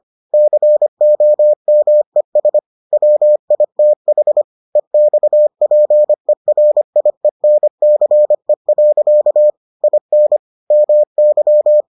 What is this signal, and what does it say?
Morse code, heard as a keyed tone, about 25 wpm, 600 Hz. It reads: COMES WITH EXPERIENCE. IN MY